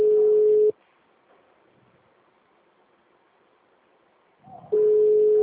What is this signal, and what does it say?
Ringback tone (Europe, 425 Hz).
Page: Dracena 2 - SP